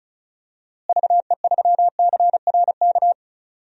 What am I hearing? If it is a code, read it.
Morse code, heard as a keyed tone, about 36 wpm, 700 Hz. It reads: VE3CRK
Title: DXCC-100 Call Signs - 6 Characters - Repeated 3x - 35wpm